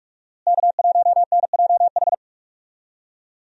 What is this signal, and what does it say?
Morse code, heard as a keyed tone, about 42 wpm, 700 Hz. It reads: K1NJH
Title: USA Call Signs - 5 Characters - Repeated 3x - 45wpm